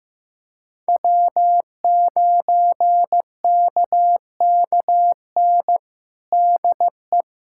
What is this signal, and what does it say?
Morse code, heard as a keyed tone, about 15 wpm, 700 Hz. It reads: W9KKN DE
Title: Practice QSO - Exchange 4 - Ending - 15wpm